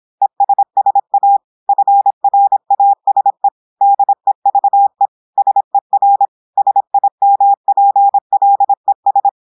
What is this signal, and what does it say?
Morse code, heard as a keyed tone, about 26 wpm, 800 Hz. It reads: ESSA FRASE DEVE SER SIMPLES
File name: portugais texte en clair 2 vitesse 26 mots minutes